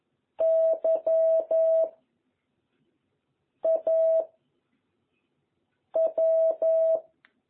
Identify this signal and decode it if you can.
Morse code, heard as a keyed tone, about 11 wpm, 660 Hz. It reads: Y A W
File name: jh8cbh / 2021cw